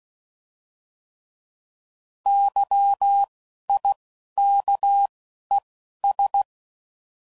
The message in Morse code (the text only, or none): YIKES